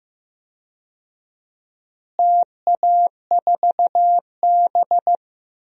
Morse code, heard as a keyed tone, about 15 wpm, 700 Hz. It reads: TA4B
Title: DXCC-100 Call Signs - 4 Characters - Repeated 2x - 15wpm